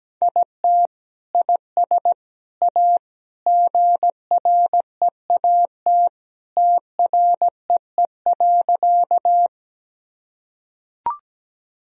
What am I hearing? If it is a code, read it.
Morse code, heard as a keyed tone, about 17 wpm, 700 Hz. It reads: IT IS A GREAT TREE.
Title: Sentences from Top 1000 Words - Review Part 1 - 17wpm